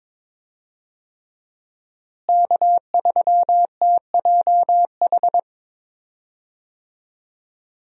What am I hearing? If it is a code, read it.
Morse code, heard as a keyed tone, about 22 wpm, 700 Hz. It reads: K3TJH
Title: USA Call Signs - 5 Characters - Repeated 3x - 22wpm